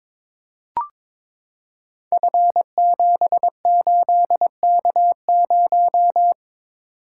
Morse code, heard as a keyed tone, about 22 wpm, 700 Hz. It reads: F78K0
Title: Code Groups - 5 Letter:Numbers - 22wpm